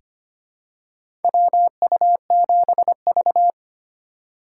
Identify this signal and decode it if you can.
Morse code, heard as a keyed tone, about 25 wpm, 700 Hz. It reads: WU7V
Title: USA Call Signs - 4 Characters - 8x Character Spacing - 25wpm